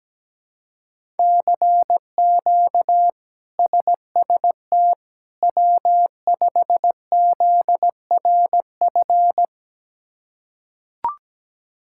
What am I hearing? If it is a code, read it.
Morse code, heard as a keyed tone, about 17 wpm, 700 Hz. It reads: CQ SST W5ZRF
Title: K1USN SST QSO Practice - 17wpm